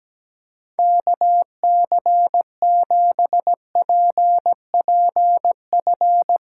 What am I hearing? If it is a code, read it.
Morse code, heard as a keyed tone, about 17 wpm, 700 Hz. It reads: KC7PPF
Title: USA Callsign DE Callsign - 6 Characters - 17wpm